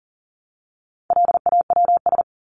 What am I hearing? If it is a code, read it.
Morse code, heard as a keyed tone, about 40 wpm, 700 Hz. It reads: LAWS
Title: ES:S Suffix - Top 100 Words with Base Words 40wpm